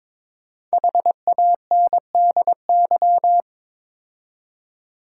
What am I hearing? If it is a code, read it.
Morse code, heard as a keyed tone, about 22 wpm, 700 Hz. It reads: HANDY
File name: 5 Letter Words - First 3 Letters Given - 22wpm